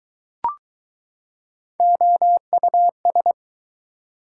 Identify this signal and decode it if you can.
Morse code, heard as a keyed tone, about 20 wpm, 700 Hz. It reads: OUS